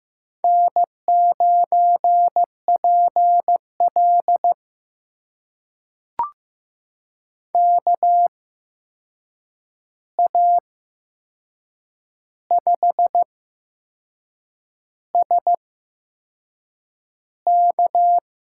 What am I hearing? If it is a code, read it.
Morse code, heard as a keyed tone, about 15 wpm, spaced as for 6 wpm, 700 Hz. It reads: N9PL KA5SK
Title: USA Call Signs - 8x Character Spacing - 15wpm